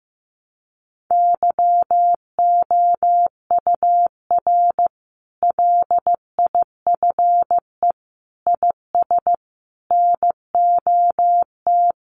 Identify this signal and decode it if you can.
Morse code, heard as a keyed tone, about 15 wpm, 700 Hz. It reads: YOUR LIFE IS NOT
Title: Sentences from Top 1200 Words 15wpm